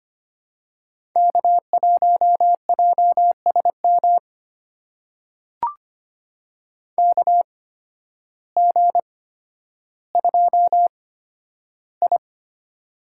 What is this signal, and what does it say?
Morse code, heard as a keyed tone, about 25 wpm, spaced as for 12 wpm, 700 Hz. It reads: K1JSM KG2I
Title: USA Call Signs - 8x Character Spacing - 25wpm